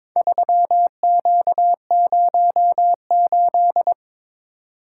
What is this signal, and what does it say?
Morse code, heard as a keyed tone, about 22 wpm, 700 Hz. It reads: 3Q08
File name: practice / 22wpm / Four Letter-Number Combinations - 22wpm